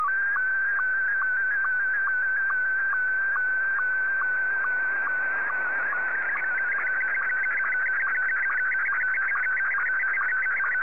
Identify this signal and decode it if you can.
SSTV (slow-scan television), Scottie 1.